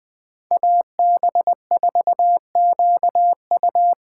Morse code, heard as a keyed tone, about 20 wpm, 700 Hz. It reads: AB4QU